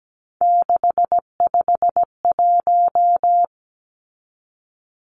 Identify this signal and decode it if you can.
Morse code, heard as a keyed tone, about 17 wpm, 700 Hz. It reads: 651
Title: Numbers - 3-Digit 17wpm